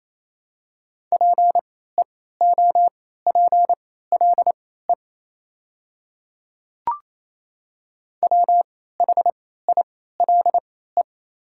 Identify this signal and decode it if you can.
Morse code, heard as a keyed tone, about 28 wpm, spaced as for 16 wpm, 700 Hz. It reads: PEOPLE WHILE
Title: 3x Character Spacing - Top 300 Words - 28wpm